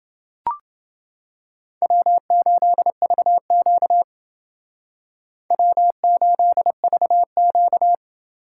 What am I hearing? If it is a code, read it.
Morse code, heard as a keyed tone, about 29 wpm, 700 Hz. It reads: W8VQ W8VQ